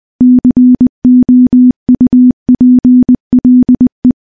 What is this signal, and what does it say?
Morse code, heard as a keyed tone, about 20 wpm, 250 Hz. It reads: COUPLE
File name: morse-training-20WPM